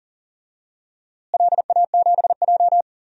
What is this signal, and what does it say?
Morse code, heard as a keyed tone, about 39 wpm, 700 Hz. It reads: LA7J
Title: DXCC-100 Call Signs - 4 Characters - Repeated 3x - 40wpm